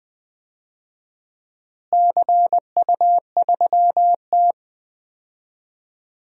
Morse code, heard as a keyed tone, about 20 wpm, 700 Hz. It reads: CU3T
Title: DXCC-100 Call Signs - 4 Characters - Repeated 3x - 20wpm